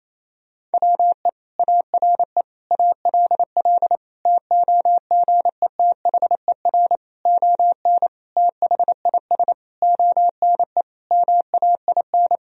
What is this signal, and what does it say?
Morse code, heard as a keyed tone, about 28 wpm, 700 Hz. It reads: WE ARE ALL TOGETHER ON THIS ONE MAIN
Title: Sentences from Top 1000 Words - Review Part 2 - 28wpm